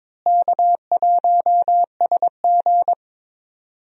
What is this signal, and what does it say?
Morse code, heard as a keyed tone, about 22 wpm, 700 Hz. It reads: K1SG
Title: USA Call Signs - 4 Characters - Repeated 2x - 22wpm